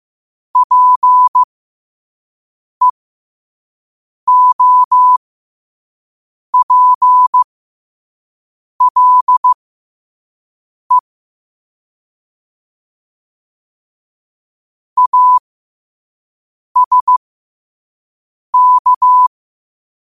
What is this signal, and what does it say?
Morse code, heard as a keyed tone, about 15 wpm, spaced as for 5 wpm, 1000 Hz. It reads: PEOPLE ASK
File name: qotdfiles / 5wpm